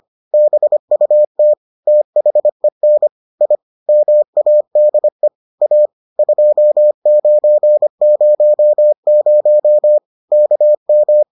Morse code, heard as a keyed tone, about 25 wpm, 600 Hz. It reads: BUT THEN I MADE A 2900 KM